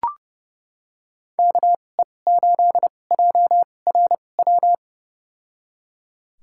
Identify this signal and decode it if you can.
Morse code, heard as a keyed tone, about 29 wpm, spaced as for 21 wpm, 700 Hz. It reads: KE8JRW
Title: USA Call Signs - 2x Character Spacing - 30wpm